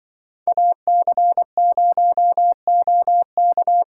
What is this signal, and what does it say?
Morse code, heard as a keyed tone, about 25 wpm, 700 Hz. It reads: AC0OK
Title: Speed Racing - USA Call Signs 20wpm